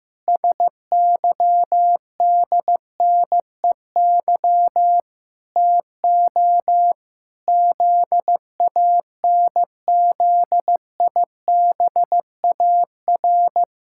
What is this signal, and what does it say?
Morse code, heard as a keyed tone, about 15 wpm, 700 Hz. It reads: SYDNEY TO ZANZIBAR
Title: Pangram Challenge v2 - 15wpm